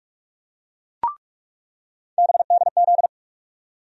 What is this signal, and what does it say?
Morse code, heard as a keyed tone, about 46 wpm, 700 Hz. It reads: BDZ
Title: USA Call Sign Postfixes - 3 Letters 45wpm